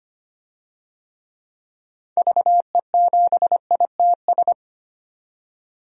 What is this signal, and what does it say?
Morse code, heard as a keyed tone, about 25 wpm, 700 Hz. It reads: VE7ITS
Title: DXCC-100 Call Signs - 6 Characters - 25wpm